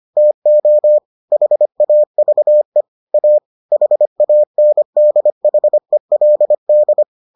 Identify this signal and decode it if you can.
Morse code, heard as a keyed tone, about 25 wpm, 600 Hz. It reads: TO HAVE A HANDHELD